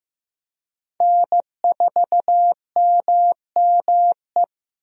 Morse code, heard as a keyed tone, about 15 wpm, 700 Hz. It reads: N4MME
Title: USA Call Signs - 5 Characters - Repeated 3x - 15wpm